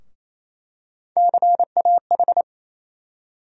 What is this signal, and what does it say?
Morse code, heard as a keyed tone, about 28 wpm, 700 Hz. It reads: CAH
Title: USA Call Sign Postfixes - 3 Letters 28wpm